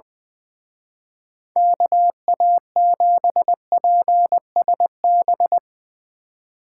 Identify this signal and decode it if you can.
Morse code, heard as a keyed tone, about 22 wpm, 700 Hz. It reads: KA7PSB
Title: Speed Racing - USA Call Signs 20wpm